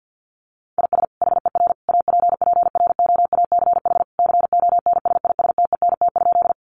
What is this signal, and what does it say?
Morse code, heard as a keyed tone, about 48 wpm, 700 Hz. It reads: IS HER APPROACH CONSISTENT?
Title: Sentences from Top 1500 Words 50wpm